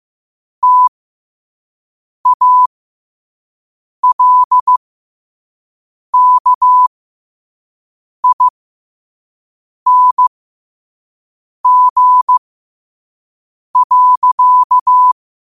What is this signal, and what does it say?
Morse code, heard as a keyed tone, about 15 wpm, spaced as for 5 wpm, 1000 Hz. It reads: TALKING.
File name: qotdfiles / 5wpm